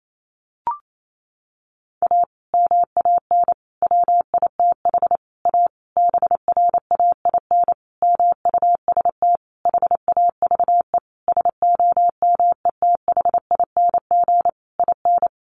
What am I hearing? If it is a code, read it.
Morse code, heard as a keyed tone, about 28 wpm, 700 Hz. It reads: A MAN WITH A BRAIN MUST HAVE SOMETHING IN
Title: Sentences from Top 1100 Words 28wpm